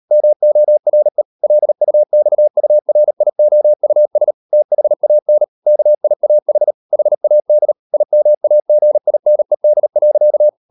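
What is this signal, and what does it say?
Morse code, heard as a keyed tone, about 38 wpm, 600 Hz. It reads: MORE LUXURIOUS THAN KIAH HAD IMAGINED.
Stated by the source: Morse code practice at 38 WPM, sending Ken Follett's novel Never in English, from CW med Gnister.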